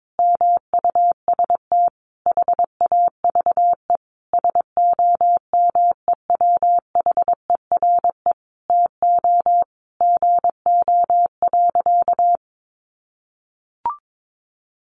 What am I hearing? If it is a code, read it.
Morse code, heard as a keyed tone, about 22 wpm, 700 Hz. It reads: MUST HAVE SOMEWHERE TO GO.
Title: Sentences from Top 1400 Words 22wpm